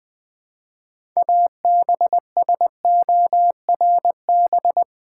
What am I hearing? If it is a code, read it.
Morse code, heard as a keyed tone, about 20 wpm, 700 Hz. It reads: ABSORB
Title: 6 Letter Words - First 3 Letters Given - 20wpm